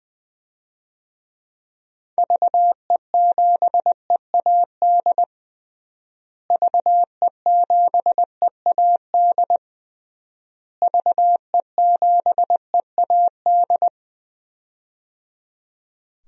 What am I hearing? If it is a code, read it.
Morse code, heard as a keyed tone, about 20 wpm, 700 Hz. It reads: VE7EAD VE7EAD VE7EAD
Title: DXCC-100 Call Signs - 6 Characters - Repeated 3x - 20wpm